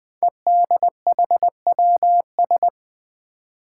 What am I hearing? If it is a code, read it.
Morse code, heard as a keyed tone, about 20 wpm, 700 Hz. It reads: EDHWS
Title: Five Challenging Letter-Number Combinations 20wpm